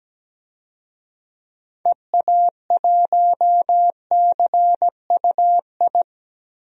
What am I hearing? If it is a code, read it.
Morse code, heard as a keyed tone, about 17 wpm, 700 Hz. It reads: EA1CUI
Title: DXCC-100 Call Signs - 6 Characters - 17wpm